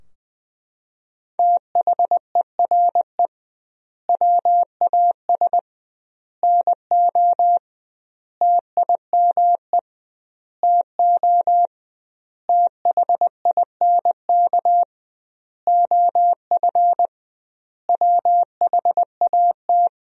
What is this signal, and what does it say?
Morse code, heard as a keyed tone, about 21 wpm, 700 Hz. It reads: THERE WAS NO TIME TO THINK OF WHAT
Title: 2x Word-Spacing - Sentences from Top 400 Words - 20wpm